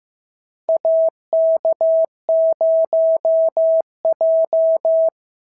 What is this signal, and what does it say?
Morse code, heard as a keyed tone, about 15 wpm, 650 Hz. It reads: AK0J
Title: SOTA QSO Practice 15wpm